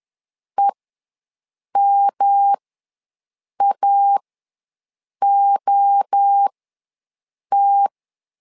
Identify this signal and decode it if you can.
Morse code, heard as a keyed tone, about 10 wpm, spaced as for 6 wpm, 800 Hz. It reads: EMAOT
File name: LEZIONI / 3 MWRO / 3 MWRO ws 10wpm 2310